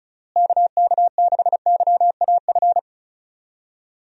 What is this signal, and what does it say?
Morse code, heard as a keyed tone, about 34 wpm, 700 Hz. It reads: KK6YAF
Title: USA Call Signs - 6 Characters - Repeated 3x - 35wpm